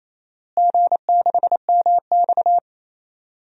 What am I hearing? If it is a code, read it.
Morse code, heard as a keyed tone, about 28 wpm, 700 Hz. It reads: G6MX
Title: DXCC-100 Call Signs - 4 Characters - Repeated 3x - 28wpm